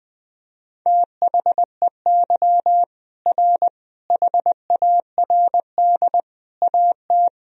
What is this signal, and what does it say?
Morse code, heard as a keyed tone, about 20 wpm, 700 Hz. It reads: THEY R HARD AT
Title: Sentences from Top 300 Words and Abbreviations 20wpm